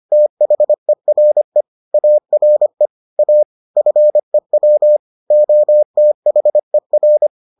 Morse code, heard as a keyed tone, about 25 wpm, 600 Hz. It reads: THERE ARE A FEW OTHER